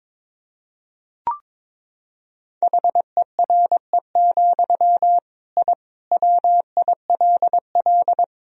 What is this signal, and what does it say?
Morse code, heard as a keyed tone, about 22 wpm, 700 Hz. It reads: HERE, I WILL
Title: Sentences from Top 1000 Words - Review Part 1 - 22wpm